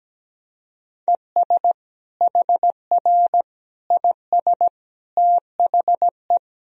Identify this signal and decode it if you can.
Morse code, heard as a keyed tone, about 17 wpm, 700 Hz. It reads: ES HR IS THE
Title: Sentences from Top 500 Words and Abbreviations 17wpm